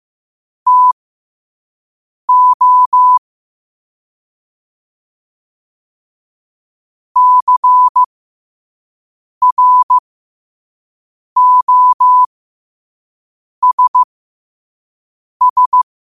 Morse code, heard as a keyed tone, about 15 wpm, spaced as for 5 wpm, 1000 Hz. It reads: TO CROSS